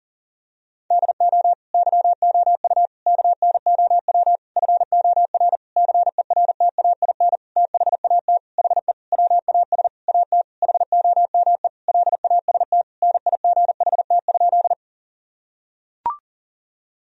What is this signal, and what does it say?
Morse code, heard as a keyed tone, about 40 wpm, 700 Hz. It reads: DO YOU KNOW FOR CERTAIN THAT HE WAS AT HOME LAST NIGHT?
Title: Sentences from Top 1000 Words - Review Part 2 - 40wpm